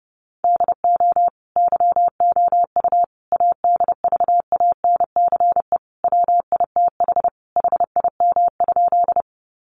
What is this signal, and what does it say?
Morse code, heard as a keyed tone, about 30 wpm, 700 Hz. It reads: DO YOU ADVANCE WITH HIM?
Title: Sentences from Top 1400 Words 30wpm